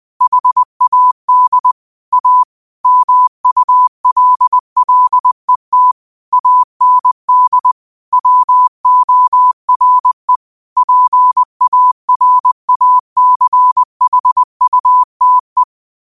Morse code, heard as a keyed tone, about 20 wpm, 1000 Hz. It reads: HAD A MULLET AND WORE PARACHUTE